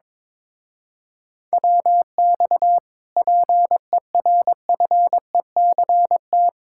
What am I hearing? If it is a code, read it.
Morse code, heard as a keyed tone, about 22 wpm, 700 Hz. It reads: WX PERFECT
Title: Speed-Racing Common QSO Phrases - 22wpm